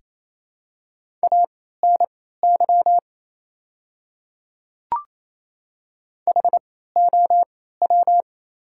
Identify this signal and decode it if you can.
Morse code, heard as a keyed tone, about 28 wpm, spaced as for 16 wpm, 700 Hz. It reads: ANY HOW
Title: 3x Character Spacing - Top 200 Words - 28wpm